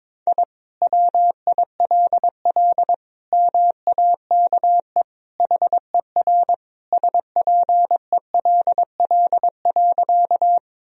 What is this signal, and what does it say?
Morse code, heard as a keyed tone, about 22 wpm, 700 Hz. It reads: I WILL MAKE HER SPELL.
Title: Sentences from Top 1000 Words - Review Part 1 - 22wpm